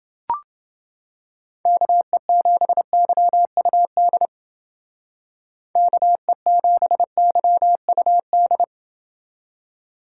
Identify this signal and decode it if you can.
Morse code, heard as a keyed tone, about 28 wpm, 700 Hz. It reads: KE7YUD KE7YUD